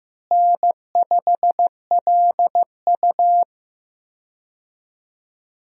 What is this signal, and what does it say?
Morse code, heard as a keyed tone, about 16 wpm, 700 Hz. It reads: N5LU
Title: USA Call Signs - 4 Characters - Repeated 3x - 15wpm